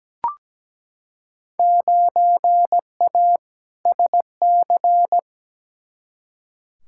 Morse code, heard as a keyed tone, about 17 wpm, 700 Hz. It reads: 9A SC
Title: Field Day Exchanges Encore - 17wpm